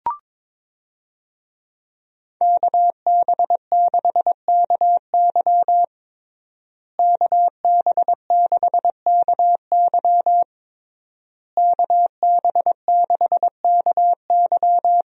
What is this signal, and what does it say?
Morse code, heard as a keyed tone, about 22 wpm, 700 Hz. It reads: KB6KY KB6KY KB6KY
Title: USA Call Signs - 5 Characters - Repeated 3x - 22wpm